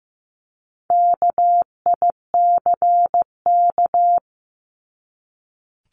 Morse code, heard as a keyed tone, about 15 wpm, 700 Hz. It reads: KICK
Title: Top 1251-1500 Words 15wpm